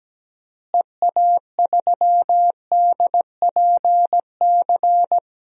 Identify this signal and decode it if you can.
Morse code, heard as a keyed tone, about 17 wpm, 700 Hz. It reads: EA3DPC
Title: DXCC-100 Call Signs - 6 Characters - Repeated 3x - 17wpm